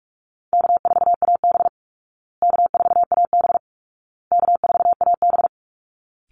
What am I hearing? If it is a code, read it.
Morse code, heard as a keyed tone, about 44 wpm, 700 Hz. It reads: K4AB K4AB K4AB
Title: Call Signs Repeated 2x - 45wpm